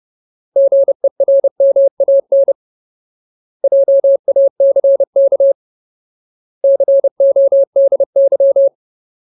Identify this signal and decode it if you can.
Morse code, heard as a keyed tone, about 30 wpm, 550 Hz. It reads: GERMAN JACK CODY
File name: Ultimate Word List / 30WPM Ultimate